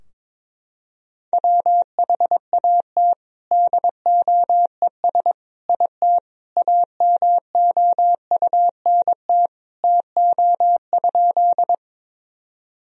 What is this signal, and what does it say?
Morse code, heard as a keyed tone, about 22 wpm, 700 Hz. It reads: WHAT DOES IT AMOUNT TO?
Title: Sentences from Top 1000 Words - Review Part 4 - 22wpm